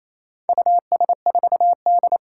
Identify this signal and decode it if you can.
Morse code, heard as a keyed tone, about 29 wpm, 700 Hz. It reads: US4D
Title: Four Challenging Letter-Number Combinations 28wpm